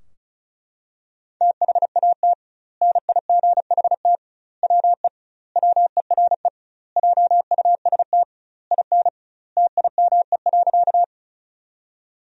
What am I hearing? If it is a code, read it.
Morse code, heard as a keyed tone, about 36 wpm, 700 Hz. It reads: THAT NIGHT WE WERE JUST IN TIME.
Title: 2x Word-Spacing - Sentences from Top 300 Words - 35wpm